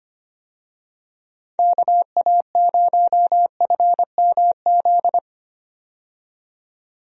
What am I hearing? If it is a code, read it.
Morse code, heard as a keyed tone, about 25 wpm, 700 Hz. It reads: KA0FMZ
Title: USA Call Signs - 6 Characters - 25wpm